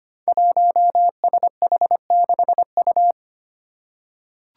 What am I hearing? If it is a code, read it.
Morse code, heard as a keyed tone, about 25 wpm, 700 Hz. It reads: 1SH6U
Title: Five Challenging Letter-Number Combinations 25wpm